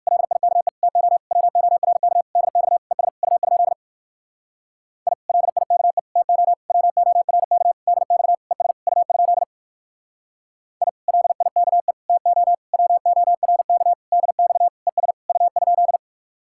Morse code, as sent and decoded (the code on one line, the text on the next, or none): .-.. .. -.- . / - --- / .-- --- .-. -.- / -.. -..- / . ... / ..- ..--.. / .. / .-.. .. -.- . / - --- / .-- --- .-. -.- / -.. -..- / . ... / ..- ..--.. / .. / .-.. .. -.- . / - --- / .-- --- .-. -.- / -.. -..- / . ... / ..- ..--..
LIKE TO WORK DX ES U? I LIKE TO WORK DX ES U? I LIKE TO WORK DX ES U?